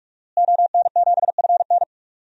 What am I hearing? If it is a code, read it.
Morse code, heard as a keyed tone, about 46 wpm, 700 Hz. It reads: ON7FN
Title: DXCC-100 Call Signs - 5 Characters - Repeated 2x - 45wpm